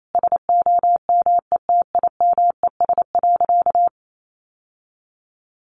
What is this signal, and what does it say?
Morse code, heard as a keyed tone, about 28 wpm, 700 Hz. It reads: SOMETIMES.
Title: Sentences from Top 600 Words 28wpm